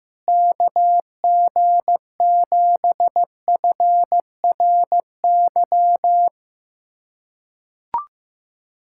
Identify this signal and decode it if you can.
Morse code, heard as a keyed tone, about 15 wpm, 700 Hz. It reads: KG7FRY
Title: USA Callsign DE Callsign - 6 Characters - 15wpm